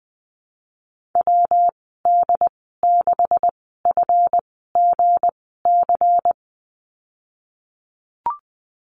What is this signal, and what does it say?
Morse code, heard as a keyed tone, about 20 wpm, spaced as for 14 wpm, 700 Hz. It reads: WD6FGC
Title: 179 Lesson 29g - Slash - Call Signs 20wpm